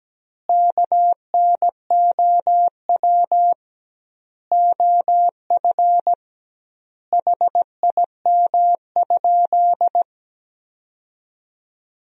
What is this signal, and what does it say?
Morse code, heard as a keyed tone, about 17 wpm, 700 Hz. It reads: KNOW OF HIM?
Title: 2x Word-Spacing - Sentences from Top 100 Words - 17wpm